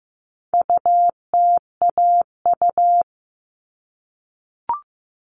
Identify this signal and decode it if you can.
Morse code, heard as a keyed tone, about 15 wpm, 700 Hz. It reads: UTAU
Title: All Cut Numbers - 4-Digit 15wpm